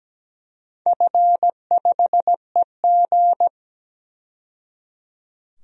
Morse code, heard as a keyed tone, about 17 wpm, 700 Hz. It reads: F5EG
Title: DXCC-100 Call Signs - 4 Characters - 17wpm